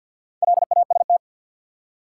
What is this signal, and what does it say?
Morse code, heard as a keyed tone, about 53 wpm, 700 Hz. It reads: LAST
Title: Four Letter Words - 50wpm